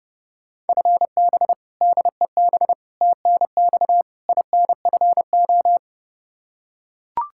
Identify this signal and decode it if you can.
Morse code, heard as a keyed tone, about 30 wpm, 700 Hz. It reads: FB DEB TNX INFO
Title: Practice QSO - Exchange 3 - More Info - 30wpm